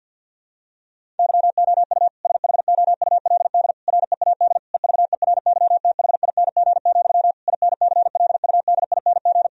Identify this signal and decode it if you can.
Morse code, heard as a keyed tone, about 50 wpm, 700 Hz. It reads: YOU SHOULD READ EVERYTHING, INCLUDING
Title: Sentences from Top 1000 Words - Review Part 4 - 50wpm